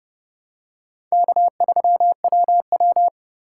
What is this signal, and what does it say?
Morse code, heard as a keyed tone, about 32 wpm, 700 Hz. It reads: K3WW
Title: USA Call Signs - 2x Character Spacing - 30wpm